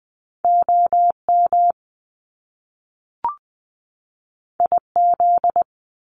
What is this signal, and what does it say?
Morse code, heard as a keyed tone, about 20 wpm, 700 Hz. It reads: O M I Z
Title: Top Two Letter Combinations in Words 20wpm